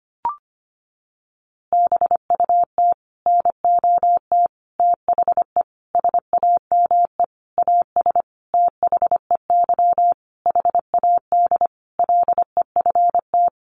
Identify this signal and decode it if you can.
Morse code, heard as a keyed tone, about 25 wpm, 700 Hz. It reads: BUT NOT THE SAME AS THEY HAD LEFT
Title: A Sound Of Thunder 25wpm